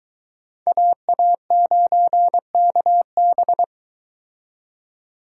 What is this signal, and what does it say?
Morse code, heard as a keyed tone, about 22 wpm, 700 Hz. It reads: AA9KB